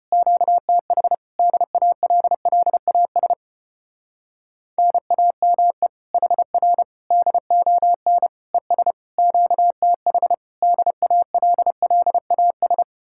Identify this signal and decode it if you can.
Morse code, heard as a keyed tone, about 33 wpm, 700 Hz. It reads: QTH DALLAS NAME HR DON ES QTH DALLAS